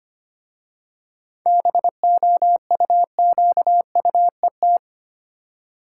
Morse code, heard as a keyed tone, about 25 wpm, 700 Hz. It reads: BOUQUET
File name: practice / 25wpm / Easy Words with Hints - 25wpm